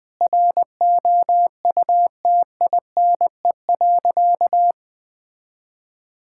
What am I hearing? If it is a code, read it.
Morse code, heard as a keyed tone, about 21 wpm, 700 Hz. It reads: ROUTINE.
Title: Difficult Sentences with Hints - 20wpm